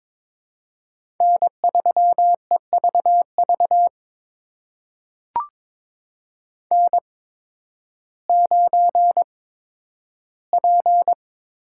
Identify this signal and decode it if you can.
Morse code, heard as a keyed tone, about 22 wpm, spaced as for 12 wpm, 700 Hz. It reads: N3EVV N9P
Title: USA Call Signs - 5 Characters - 8x Character Spacing - 22wpm